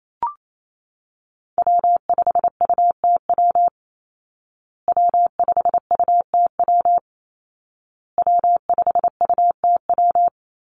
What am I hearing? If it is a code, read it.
Morse code, heard as a keyed tone, about 28 wpm, 700 Hz. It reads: W5UTW W5UTW W5UTW
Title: Call Signs Repeated 2x - 28wpm